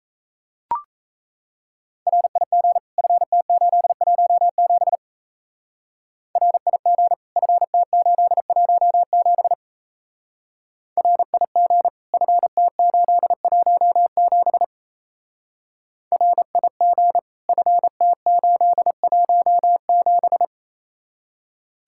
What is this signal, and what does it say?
Morse code, heard as a keyed tone, about 33 wpm, 700 Hz. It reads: RIG FT817 RIG FT817 RIG FT817 RIG FT817
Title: Speed-Racing Common QSO Phrases - 28wpm